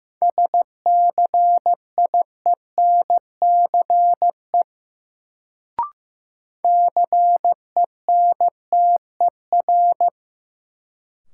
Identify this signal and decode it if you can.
Morse code, heard as a keyed tone, about 15 wpm, 700 Hz. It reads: SCIENCE CENTER
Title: Sentences from Top 1000 Words - Review Part 2 - New Word Drill - 15wpm